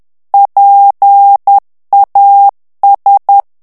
Morse code, heard as a keyed tone, about 10 wpm, 800 Hz. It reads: PAS
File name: morse mp3 / FRANCAIS TEXTE CLAIR 2 VITESSE 10 MOTS MINUTES